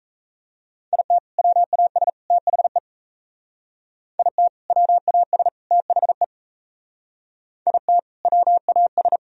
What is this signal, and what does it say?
Morse code, heard as a keyed tone, about 35 wpm, 700 Hz. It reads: IT WAS THE IT WAS THE IT WAS
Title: Speed-Racing - Sets of 3 Words 28wpm